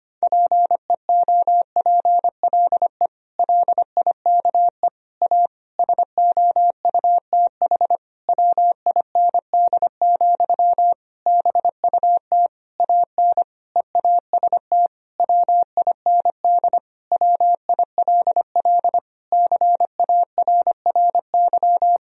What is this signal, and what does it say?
Morse code, heard as a keyed tone, about 25 wpm, 700 Hz. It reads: PEOPLE LIKE A SOUTH WIND, BUT AN EAST WIND WILL CARRY
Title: Sentences from Top 1000 Words - Review Part 3 - 25wpm